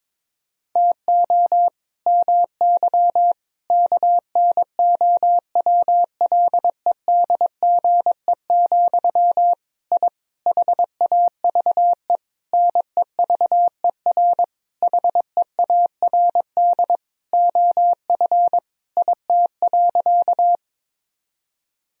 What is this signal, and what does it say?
Morse code, heard as a keyed tone, about 22 wpm, 700 Hz. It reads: TO MY KNOWLEDGE, I HAVE NEVER HEARD OF IT.
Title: Sentences from Top 1000 Words - Review Part 3 - 22wpm